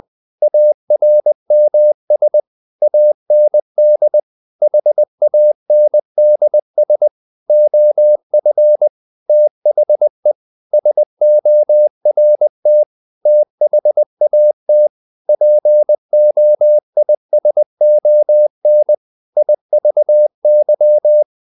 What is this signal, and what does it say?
Morse code, heard as a keyed tone, about 20 wpm, 600 Hz. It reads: ARMS AND HANDS OF THE SORT THAT POISON IVY
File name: the old man and the sea / the old man and the sea part 1 20wpm